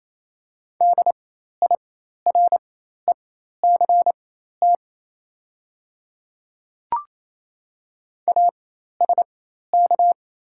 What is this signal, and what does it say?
Morse code, heard as a keyed tone, about 28 wpm, spaced as for 13 wpm, 700 Hz. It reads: DIRECT ASK
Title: 4x Character Spacing - Top 500 Words - 28wpm